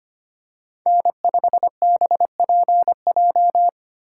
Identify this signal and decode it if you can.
Morse code, heard as a keyed tone, about 25 wpm, 700 Hz. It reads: N5BPJ
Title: USA Call Signs - 5 Characters - 4x Character Spacing - 25wpm